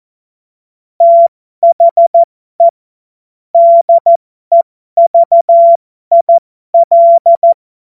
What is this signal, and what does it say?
Morse code, heard as a keyed tone, about 14 wpm, spaced as for 12 wpm, 680 Hz. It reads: THE DEVIL
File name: Morsekurs / Witze